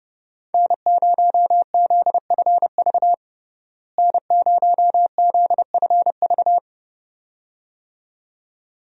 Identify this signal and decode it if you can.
Morse code, heard as a keyed tone, about 32 wpm, 700 Hz. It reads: N0ZFV N0ZFV
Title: USA Call Signs - 5 Characters - Repeated 2x - 30wpm